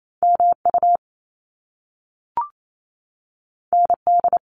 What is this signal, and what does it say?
Morse code, heard as a keyed tone, about 28 wpm, 700 Hz. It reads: MU ND
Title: Top Two Letter Combinations in Words 28wpm